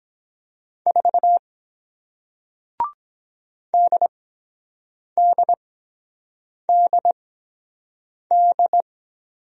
Morse code, heard as a keyed tone, about 23 wpm, spaced as for 7 wpm, 700 Hz. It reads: 4 DDDD